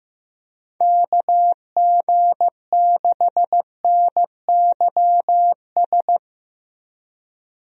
Morse code, heard as a keyed tone, about 15 wpm, 700 Hz. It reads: KG6NYS
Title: USA Call Signs - 6 Characters - Repeated 3x - 15wpm